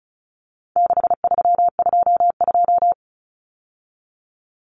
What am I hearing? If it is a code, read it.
Morse code, heard as a keyed tone, about 34 wpm, 700 Hz. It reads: 6322
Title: Numbers - 4-Digit 35wpm